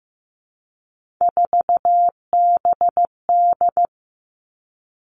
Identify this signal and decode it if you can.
Morse code, heard as a keyed tone, about 15 wpm, 700 Hz. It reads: 4BD
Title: Three Challenging Letter-Number Combinations 15wpm